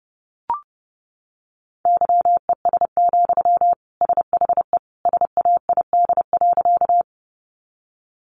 Morse code, heard as a keyed tone, about 30 wpm, 700 Hz. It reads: YES, SHE SAID.
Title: Sentences from Top 500 Words 30wpm